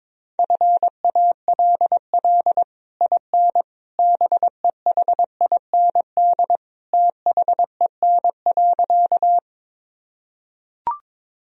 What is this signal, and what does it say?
Morse code, heard as a keyed tone, about 22 wpm, 700 Hz. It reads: FALL IN BEHIND THEN.
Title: Sentences from Top 1000 Words - Review Part 2 - 22wpm